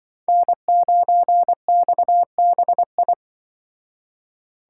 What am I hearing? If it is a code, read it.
Morse code, heard as a keyed tone, about 23 wpm, 700 Hz. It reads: N9XBI
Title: Speed Racing - USA 1x3 Call Signs 20wpm